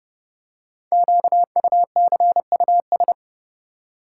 Morse code, heard as a keyed tone, about 28 wpm, 700 Hz. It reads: QUCUS
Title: Code Groups - 5 Letters - 30wpm